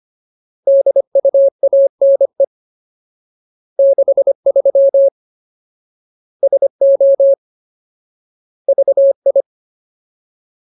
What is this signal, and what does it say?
Morse code, heard as a keyed tone, about 25 wpm, 550 Hz. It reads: DUANE 63 SO VI
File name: Ultimate Word List / 25WPM Ultimate